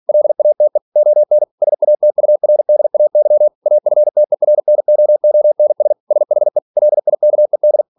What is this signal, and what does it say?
Morse code, heard as a keyed tone, about 46 wpm, 600 Hz. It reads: LATE ON SATURDAY AFTERNOONS SHE LIKED
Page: Never 47wpm